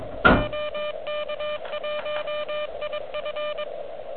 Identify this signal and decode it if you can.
Morse code, heard as a keyed tone, about 22 wpm, 3000 Hz, with noise. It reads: OK1IF
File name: OK QRP 2022 / Zvuk / REC38